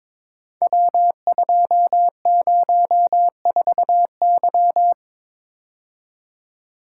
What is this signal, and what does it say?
Morse code, heard as a keyed tone, about 22 wpm, 700 Hz. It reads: W204Y
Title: Code Groups - 5 Letter:Numbers - 22wpm